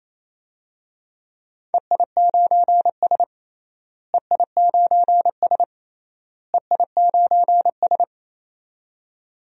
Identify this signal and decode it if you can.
Morse code, heard as a keyed tone, about 28 wpm, 700 Hz. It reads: EI9S EI9S EI9S
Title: DXCC-100 Call Signs - 4 Characters - Repeated 3x - 28wpm